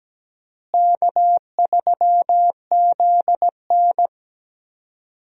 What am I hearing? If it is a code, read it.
Morse code, heard as a keyed tone, about 17 wpm, 700 Hz. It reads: K3ZN
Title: USA Call Signs - 4 Characters - Repeated 2x - 17wpm